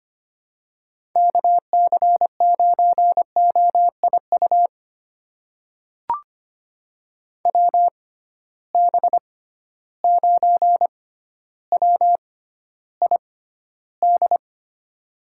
Morse code, heard as a keyed tone, about 25 wpm, spaced as for 13 wpm, 700 Hz. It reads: KC9OIU WB9WID
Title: USA Call Signs - 6 Characters - 6x Character Spacing - 25wpm